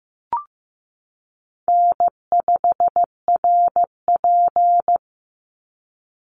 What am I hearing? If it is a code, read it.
Morse code, heard as a keyed tone, about 15 wpm, 700 Hz. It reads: N5RP
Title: Easy Call Signs Repeated 3x - 15wpm